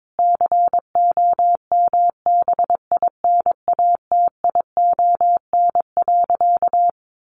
Morse code, heard as a keyed tone, about 22 wpm, 700 Hz. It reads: COMBINATION.
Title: Sentences from Top 1300 Words 22wpm